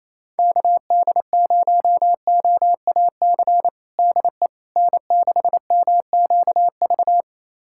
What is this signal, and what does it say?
Morse code, heard as a keyed tone, about 28 wpm, 700 Hz. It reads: KD0OAC DE N6MQV
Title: USA Callsign DE Callsign 28wpm